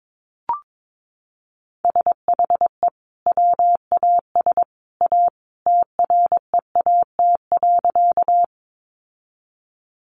Morse code, heard as a keyed tone, about 22 wpm, 700 Hz. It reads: SHE WAS A TREAT.
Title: Sentences from Top 1300 Words 22wpm